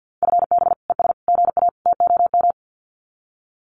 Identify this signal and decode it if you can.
Morse code, heard as a keyed tone, about 50 wpm, 700 Hz. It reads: FB ES GA TOM
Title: Common QSO Phrases Encore 50wpm